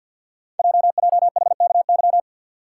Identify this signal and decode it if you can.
Morse code, heard as a keyed tone, about 50 wpm, 690 Hz. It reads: JJHKY
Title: Code Groups - 5 Letters - 50wpm